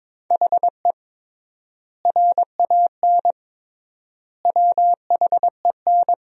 Morse code, heard as a keyed tone, about 22 wpm, 700 Hz. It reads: HE RAN WHEN
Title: 3x Word-Spacing - Sentences from Top 500 Words - 22wpm